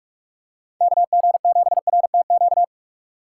Morse code, heard as a keyed tone, about 43 wpm, 700 Hz. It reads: KG7RTQ
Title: USA Call Signs - 6x Character Spacing - 45wpm